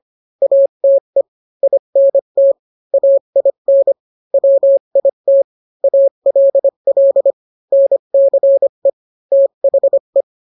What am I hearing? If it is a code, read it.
Morse code, heard as a keyed tone, about 25 wpm, 550 Hz. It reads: ATE INT AIN WIT ALL NCE THE